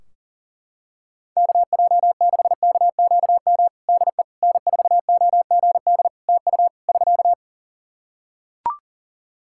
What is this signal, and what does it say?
Morse code, heard as a keyed tone, about 40 wpm, 700 Hz. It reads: KJ6KQM DE N4OGD TU <SK>
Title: Practice QSO - Exchange 4 - Ending - 40wpm